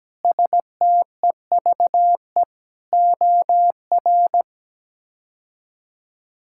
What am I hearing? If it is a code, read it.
Morse code, heard as a keyed tone, about 17 wpm, 700 Hz. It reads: STEVE OR
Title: K1USN SST QSO Practice - 17wpm